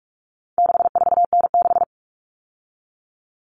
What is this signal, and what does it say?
Morse code, heard as a keyed tone, about 44 wpm, 700 Hz. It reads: 64N6
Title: All Cut Numbers - 4-Digit 45wpm